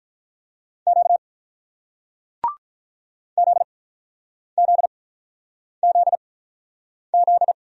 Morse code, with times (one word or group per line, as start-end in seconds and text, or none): 0.9-1.2 Q
3.4-7.5 ZZZZ